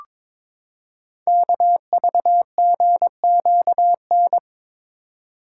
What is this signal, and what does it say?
Morse code, heard as a keyed tone, about 22 wpm, 700 Hz. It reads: KVGQN
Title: Code Groups - 5 Letters - 22wpm